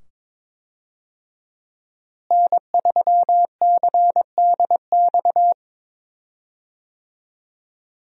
Morse code, heard as a keyed tone, about 21 wpm, 700 Hz. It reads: N3CDX
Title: USA Call Signs - 5 Characters - Repeated 2x - 22wpm